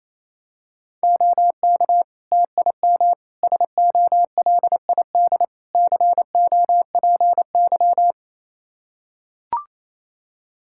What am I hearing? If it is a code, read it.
Morse code, heard as a keyed tone, about 28 wpm, 700 Hz. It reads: OK TIM SOLID COPY
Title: Practice QSO - Exchange 3 - More Info - 28wpm